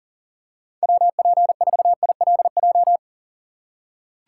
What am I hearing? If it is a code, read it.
Morse code, heard as a keyed tone, about 37 wpm, 700 Hz. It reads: WP4ILJ
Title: USA Call Signs - 6 Characters - 2x Character Spacing - 40wpm